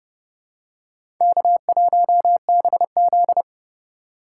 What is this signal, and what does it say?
Morse code, heard as a keyed tone, about 30 wpm, 700 Hz. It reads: K1BZ
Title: K1USN SST QSO Practice - 30wpm